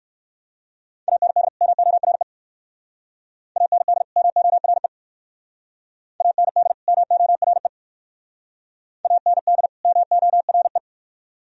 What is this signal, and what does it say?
Morse code, heard as a keyed tone, about 59 wpm, 700 Hz. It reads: AND MORE AND MORE AND MORE AND MORE